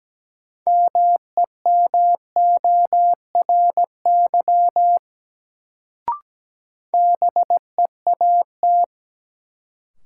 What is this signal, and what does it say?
Morse code, heard as a keyed tone, about 17 wpm, 700 Hz. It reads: MEMORY BEAT
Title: Sentences from Top 1300 Words - New Word Drill - 17wpm